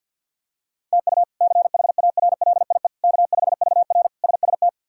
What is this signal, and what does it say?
Morse code, heard as a keyed tone, about 50 wpm, 700 Hz. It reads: TU CHARLIE K5VR SST
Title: K1USN SST QSO Practice - 50wpm